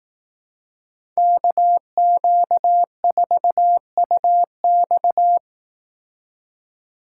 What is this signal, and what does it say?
Morse code, heard as a keyed tone, about 18 wpm, 700 Hz. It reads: KQ4UX
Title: Speed Racing - USA 2x2 Call Signs 15wpm